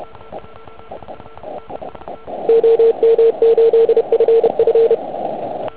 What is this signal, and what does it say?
Morse code, heard as a keyed tone, about 29 wpm, 470 Hz. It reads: OM8FF
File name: Archiv / CUC149 / OK1IF CUC149